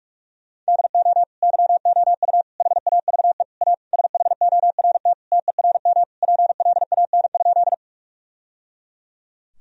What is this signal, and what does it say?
Morse code, heard as a keyed tone, about 46 wpm, 700 Hz. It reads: DO YOU HAVE A SHORT TERM PLAN?
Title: Sentences from Top 1000 Words - Review Part 4 - 45wpm